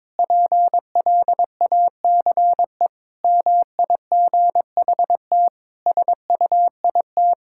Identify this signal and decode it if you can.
Morse code, heard as a keyed tone, about 22 wpm, 700 Hz. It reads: PLACE MIGHT SUIT
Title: Sentences from Top 1600 Words 22wpm